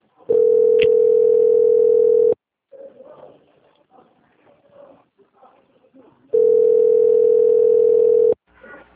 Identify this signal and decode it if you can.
Ringback tone (North America), 440 + 480 Hz, with speech.
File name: out- - - - - .8370